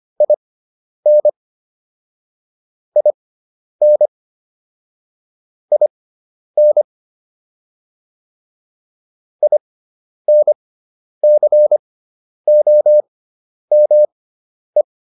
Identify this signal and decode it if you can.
Morse code, heard as a keyed tone, about 25 wpm, spaced as for 10 wpm, 600 Hz. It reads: IN IN IN INCOME